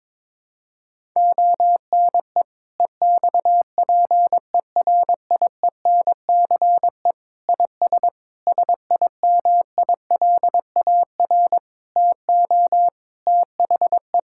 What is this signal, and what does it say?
Morse code, heard as a keyed tone, about 22 wpm, 700 Hz. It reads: ONE EXPERIENCE IS SIMILAR TO THE
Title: Sentences from Top 1000 Words - Review Part 5 - 22wpm